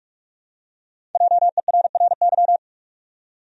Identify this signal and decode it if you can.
Morse code, heard as a keyed tone, about 48 wpm, 700 Hz. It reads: JERRY
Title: USA Common Names - Max 5 Letters - 45wpm